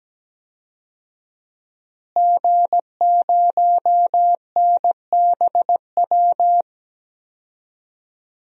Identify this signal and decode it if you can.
Morse code, heard as a keyed tone, about 17 wpm, 700 Hz. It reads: G0NBW
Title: DXCC-100 Call Signs - 5 Characters - Repeated 3x - 17wpm